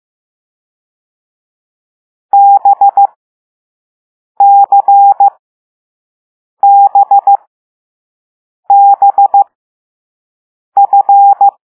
Morse code, heard as a keyed tone, about 15 wpm, spaced as for 5 wpm, 800 Hz. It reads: BCBBF
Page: Play Lesson 6 C - all letters in previous lessons